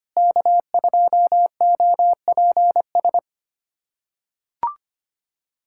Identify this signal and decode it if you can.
Morse code, heard as a keyed tone, about 25 wpm, 700 Hz. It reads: K2OPS
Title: USA Callsign DE Callsign - 5 Characters - 25wpm